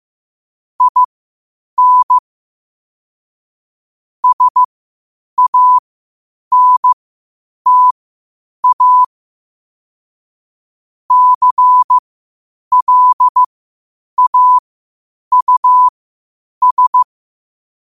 Morse code, heard as a keyed tone, about 15 wpm, spaced as for 8 wpm, 1000 Hz. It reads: IN SANTA CLAUS
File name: qotdfiles / 8wpm